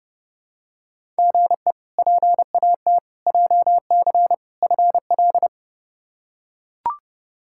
Morse code, heard as a keyed tone, about 32 wpm, 700 Hz. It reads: GE PAT JC FL
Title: K1USN SST QSO Practice - 30wpm